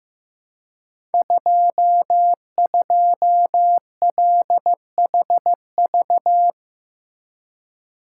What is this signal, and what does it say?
Morse code, heard as a keyed tone, about 15 wpm, 700 Hz. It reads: 22LHV
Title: Code Groups - 5 Letter:Numbers - 15wpm